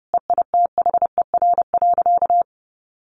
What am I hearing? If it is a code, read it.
Morse code, heard as a keyed tone, about 30 wpm, 700 Hz. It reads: EITHER.
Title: Sentences from Top 600 Words 30wpm